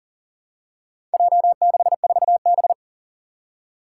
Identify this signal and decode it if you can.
Morse code, heard as a keyed tone, about 40 wpm, 700 Hz. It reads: J64B
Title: Four Challenging Letter-Number Combinations 40wpm